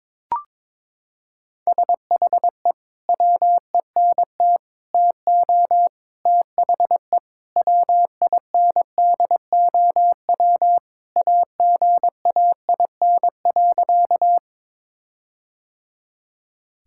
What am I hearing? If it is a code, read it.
Morse code, heard as a keyed tone, about 21 wpm, 700 Hz. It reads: SHE WENT TO THE WINDOW AGAIN.
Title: Sentences from Top 1600 Words 22wpm